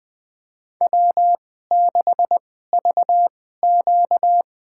Morse code, heard as a keyed tone, about 20 wpm, spaced as for 14 wpm, 700 Hz. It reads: W6VQ
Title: USA Call Signs - 4 Characters - 2x Character Spacing - 20wpm